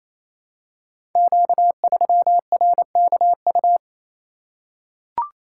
Morse code, heard as a keyed tone, about 27 wpm, 700 Hz. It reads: Q3RKU
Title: Code Groups - 5 Letter:Numbers - 28wpm